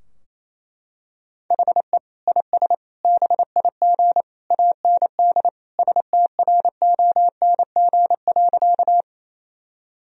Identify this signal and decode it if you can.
Morse code, heard as a keyed tone, about 28 wpm, 700 Hz. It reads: HE IS BIG AND STRONG.
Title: Sentences from Top 1000 Words - Review Part 3 - 28wpm